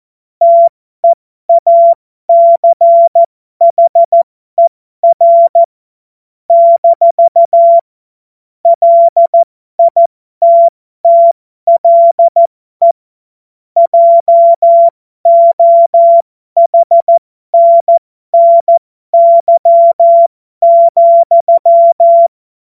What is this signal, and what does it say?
Morse code, heard as a keyed tone, about 14 wpm, spaced as for 12 wpm, 680 Hz. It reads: TEACHER - LITTLE JOHNNY,
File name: Morsekurs / Witze